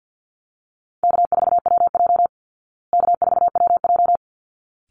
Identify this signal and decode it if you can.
Morse code, heard as a keyed tone, about 50 wpm, 700 Hz. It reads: K4WJ K4WJ
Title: Easy Call Signs Repeated 2x - 50wpm